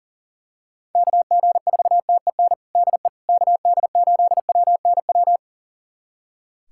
Morse code, heard as a keyed tone, about 40 wpm, 700 Hz. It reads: KG4TEN DE KD8WNW
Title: USA Callsign DE Callsign - 6 Characters - 40wpm